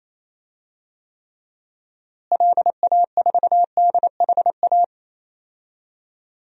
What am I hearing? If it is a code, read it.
Morse code, heard as a keyed tone, about 28 wpm, 700 Hz. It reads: LA4DHA
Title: DXCC-100 Call Signs - 6 Characters - Repeated 3x - 28wpm